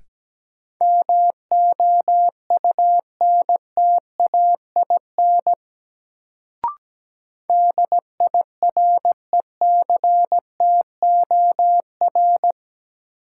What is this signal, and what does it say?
Morse code, heard as a keyed tone, about 17 wpm, 700 Hz. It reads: MOUNTAIN DIRECTOR
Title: Sentences from Top 1300 Words - New Word Drill Alt - 17wpm